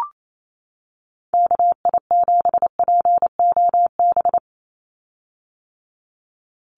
Morse code, heard as a keyed tone, about 28 wpm, 700 Hz. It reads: KI7POB
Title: USA Call Signs 28wpm